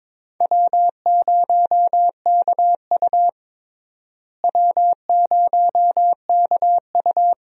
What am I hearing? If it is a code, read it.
Morse code, heard as a keyed tone, about 22 wpm, 700 Hz. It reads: W0KU W0KU
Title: USA Call Signs - 4 Characters - Repeated 2x - 22wpm